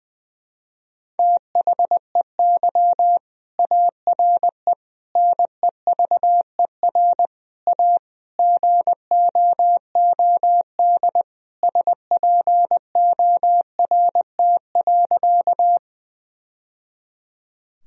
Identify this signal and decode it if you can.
Morse code, heard as a keyed tone, about 20 wpm, 700 Hz. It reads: THEY ARE NEVER A GOOD SPORT.
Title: Sentences from Top 1000 Words - Review Part 4 - 20wpm